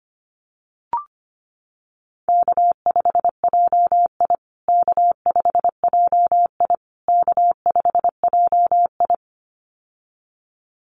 Morse code, heard as a keyed tone, about 25 wpm, 700 Hz. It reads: K5JI K5JI K5JI
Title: USA Call Signs Repeated 3x 25wpm